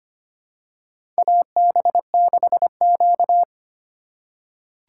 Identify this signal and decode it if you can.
Morse code, heard as a keyed tone, about 25 wpm, 700 Hz. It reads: AB6Q
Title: USA Call Signs - 4 Characters - 2x Character Spacing - 25wpm